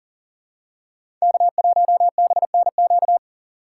Morse code, heard as a keyed tone, about 39 wpm, 700 Hz. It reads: K1BNQ
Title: USA Call Signs - 5 Characters - 2x Character Spacing - 40wpm